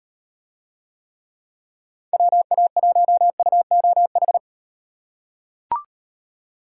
Morse code, heard as a keyed tone, about 38 wpm, 700 Hz. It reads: WA1UOH